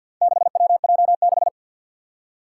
Morse code, heard as a keyed tone, about 52 wpm, 700 Hz. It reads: 6WJ6
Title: Four Challenging Letter-Number Combinations 50wpm